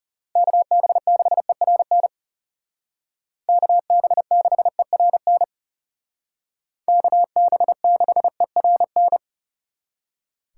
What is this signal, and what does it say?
Morse code, heard as a keyed tone, about 37 wpm, 700 Hz. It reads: KB6ERN KB6ERN KB6ERN
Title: Speed Racing - USA 2x3 Call Signs 30wpm